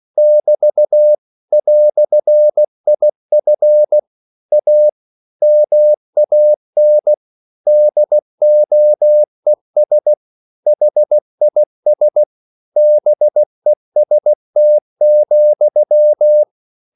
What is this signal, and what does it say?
Morse code, heard as a keyed tone, about 16 wpm, 600 Hz. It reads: <BT> "IF A MAN DOES HIS BEST,
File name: CW / Practice / EnglishQuotations0116